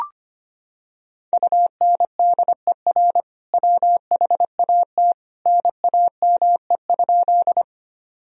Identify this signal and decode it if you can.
Morse code, heard as a keyed tone, about 25 wpm, 700 Hz. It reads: UNDER WHAT NAME?
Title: Sentences from Top 1000 Words - Review Part 1 - 25wpm